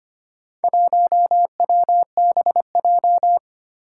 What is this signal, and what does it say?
Morse code, heard as a keyed tone, about 25 wpm, 700 Hz. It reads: 1WBJ
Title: Four Challenging Letter-Number Combinations 25wpm